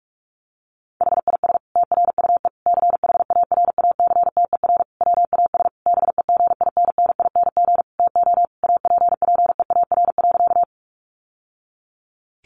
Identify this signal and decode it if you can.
Morse code, heard as a keyed tone, about 44 wpm, 700 Hz. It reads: HIS TRUE CHARACTER WAS BEGINNING TO APPEAR.
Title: Sentences from Top 1200 Words 45wpm